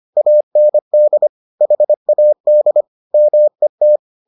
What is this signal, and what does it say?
Morse code, heard as a keyed tone, about 25 wpm, 600 Hz. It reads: AND HAD MET